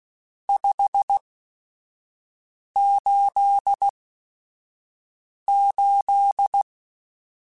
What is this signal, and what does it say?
Morse code, heard as a keyed tone, about 16 wpm, spaced as for 5 wpm, 800 Hz. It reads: 588
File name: RSTs / RST-mp3s / Various RST @5